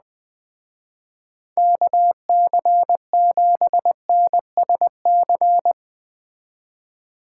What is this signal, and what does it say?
Morse code, heard as a keyed tone, about 20 wpm, 700 Hz. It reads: KC7NSC